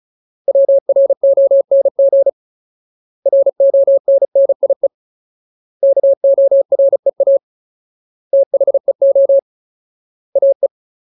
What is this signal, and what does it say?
Morse code, heard as a keyed tone, about 34 wpm, 550 Hz. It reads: WRONG RONNIE KOREA THEO AE
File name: Ultimate Word List / 35WPM Ultimate